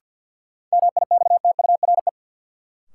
Morse code, heard as a keyed tone, about 53 wpm, 700 Hz. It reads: MIXTURE
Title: Sentences from Top 1500 Words - New Word Drill - 50wpm